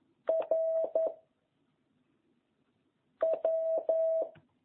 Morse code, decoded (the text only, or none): R W